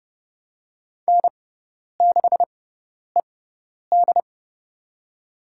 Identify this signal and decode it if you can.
Morse code, heard as a keyed tone, about 30 wpm, spaced as for 10 wpm, 700 Hz. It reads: N6ED
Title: USA Call Signs - 4 Characters - 6x Character Spacing - 30wpm